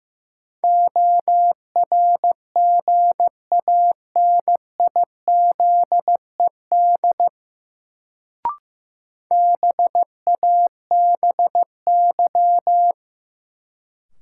Morse code, heard as a keyed tone, about 16 wpm, 700 Hz. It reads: ORGANIZED BABY
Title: Sentences from Top 1300 Words - New Word Drill - 15wpm